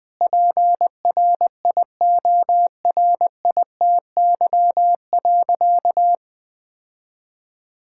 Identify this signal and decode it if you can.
Morse code, heard as a keyed tone, about 20 wpm, 700 Hz. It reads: PRIORITY.
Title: Sentences from Top 1600 Words 20wpm